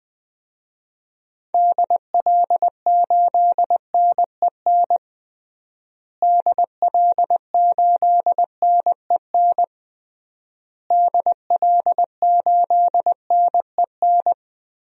Morse code, heard as a keyed tone, about 20 wpm, 700 Hz. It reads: DL8NEN DL8NEN DL8NEN
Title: DXCC-100 Call Signs - 6 Characters - Repeated 3x - 20wpm